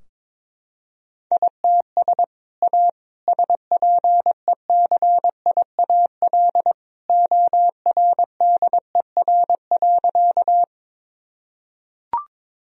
Morse code, heard as a keyed tone, about 22 wpm, 700 Hz. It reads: ITS A SPECIAL ORDER.
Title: Sentences from Top 1000 Words - Review Part 3 - 22wpm